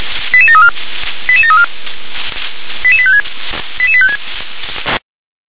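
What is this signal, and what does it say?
Five-tone selective call (ZVEI), heard as quick runs of five tones, tones of about 70 ms. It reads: 88734 88734 88856 88856
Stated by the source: Received via SDR; ZVEI five-tone selective call 88748.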